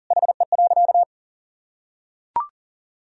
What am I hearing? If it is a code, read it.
Morse code, heard as a keyed tone, about 40 wpm, 700 Hz. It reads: HE.
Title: Sentences from Top 400 Words and Abbreviations 40wpm